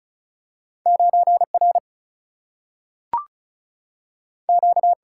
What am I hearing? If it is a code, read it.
Morse code, heard as a keyed tone, about 36 wpm, 700 Hz. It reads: 9 R Q
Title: Single-Letter Single-Number 35wpm